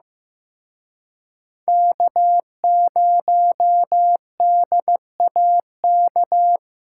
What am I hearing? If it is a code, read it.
Morse code, heard as a keyed tone, about 18 wpm, 700 Hz. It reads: K0DAK
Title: Speed Racing - USA 1x3 Call Signs 15wpm